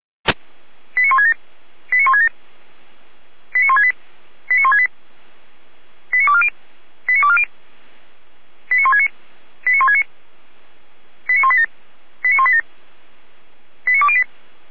Five-tone selective call (ZVEI), heard as quick runs of five tones, tones of about 70 ms. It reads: 89167 89167 89168 89168 89240 89240 89169 89169 89187 89187 89208